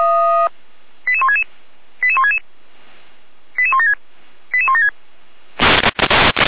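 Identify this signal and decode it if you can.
Five-tone selective call (ZVEI), heard as quick runs of five tones, tones of about 80 ms. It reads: 88166 88166 88176 88176
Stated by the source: ZVEI five-tone selective call 88307 received by SDR, feuerwehr-orlamuende fire brigade.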